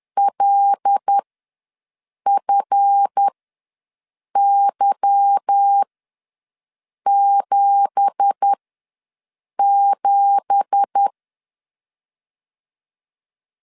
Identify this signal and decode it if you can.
Morse code, heard as a keyed tone, about 10 wpm, spaced as for 6 wpm, 800 Hz. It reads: LFY77
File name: LEZIONI / 9 2867 ws 10wpm 4215